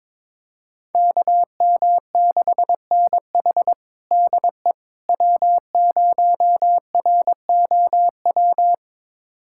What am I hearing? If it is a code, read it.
Morse code, heard as a keyed tone, about 22 wpm, 700 Hz. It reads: KM6NH DE W0ROW
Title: USA Callsign DE Callsign - 5 Characters - 22wpm